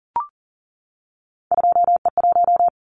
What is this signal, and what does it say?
Morse code, heard as a keyed tone, about 41 wpm, 700 Hz. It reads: 2E1
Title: Callsign Prefixes for Top 25 DXCC - 40wpm